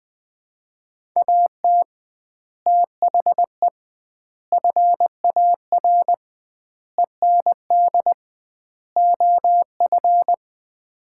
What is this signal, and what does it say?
Morse code, heard as a keyed tone, about 20 wpm, 700 Hz. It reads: AT THE FAR END OF
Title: 2x Word-Spacing - Sets of 5 Words - 20wpm